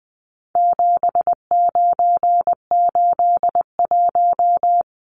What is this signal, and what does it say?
Morse code, heard as a keyed tone, about 20 wpm, 700 Hz. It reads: 7981
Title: Numbers - 4-Digit 20wpm